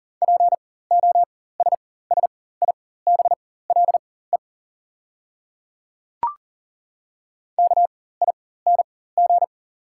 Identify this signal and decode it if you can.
Morse code, heard as a keyed tone, about 40 wpm, spaced as for 19 wpm, 690 Hz. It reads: POSSIBLE KING
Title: 4x Character Spacing - Top 500 Words - 40wpm